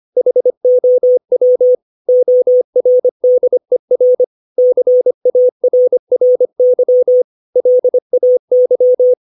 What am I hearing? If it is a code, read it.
Morse code, heard as a keyed tone, about 25 wpm, 500 Hz. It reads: HOW ORDER CARRY LAY
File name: 632 words / 632 Words 25WPM